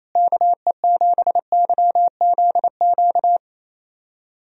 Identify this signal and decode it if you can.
Morse code, heard as a keyed tone, about 28 wpm, 700 Hz. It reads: KE7YZQ
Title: USA Call Signs - 6 Characters - Repeated 2x - 28wpm